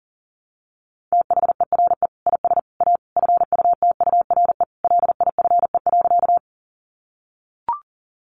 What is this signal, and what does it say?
Morse code, heard as a keyed tone, about 37 wpm, 700 Hz. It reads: THERE IS A FUTURE LIFE.
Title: Sentences from Top 700 Words 40wpm